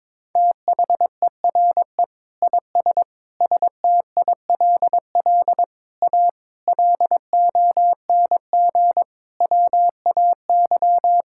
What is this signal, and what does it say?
Morse code, heard as a keyed tone, about 22 wpm, 700 Hz. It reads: THERE IS STILL A LONG WAY
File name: practice / 22wpm / Sets of 8 Words - 22wpm